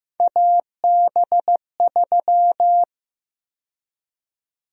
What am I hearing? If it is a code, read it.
Morse code, heard as a keyed tone, about 15 wpm, 700 Hz. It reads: AB3
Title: USA Call Sign Prefixes - 15wpm